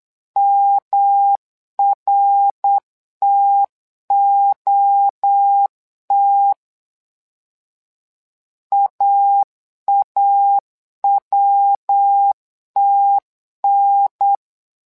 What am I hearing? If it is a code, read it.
Morse code, heard as a keyed tone, about 8 wpm, 800 Hz. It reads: MRTOT AAWTN
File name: LEZIONI / 3 MWRO / 3 MWRO 8wpm 2606